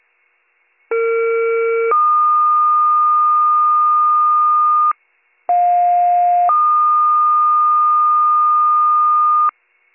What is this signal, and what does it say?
Two-tone page, heard as a one-second tone, then a three-second tone, about 460 then 1220 Hz, about 710 then 1220 Hz. It is heard from 0.9 to 9.5 s.